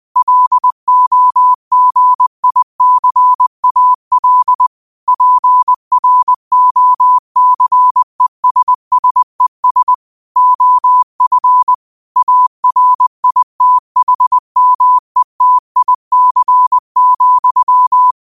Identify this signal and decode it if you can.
Morse code, heard as a keyed tone, about 20 wpm, 1000 Hz. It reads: LOGICAL PROCESSES OF ARITHMETIC,
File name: qotdfiles / 20wpm